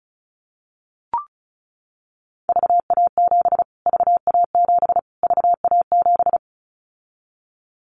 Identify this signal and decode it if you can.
Morse code, heard as a keyed tone, about 34 wpm, 700 Hz. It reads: VA7 VA7 VA7
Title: Callsign Prefixes for Top 25 DXCC - Repeated 3x - 35wpm